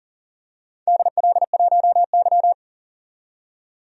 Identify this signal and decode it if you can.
Morse code, heard as a keyed tone, about 38 wpm, 700 Hz. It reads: DL1Y
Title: DXCC-100 Call Signs - 4 Characters - 40wpm